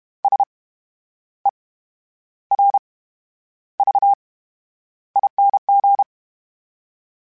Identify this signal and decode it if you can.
Morse code, heard as a keyed tone, about 32 wpm, spaced as for 11 wpm, 800 Hz. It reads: SERVING
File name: Special INGs / Special INGs-mp3s / VerySlowToFaster Ing @30